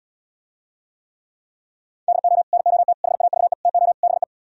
Morse code, heard as a keyed tone, about 77 wpm, 690 Hz. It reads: DO NOT HAVE TO BE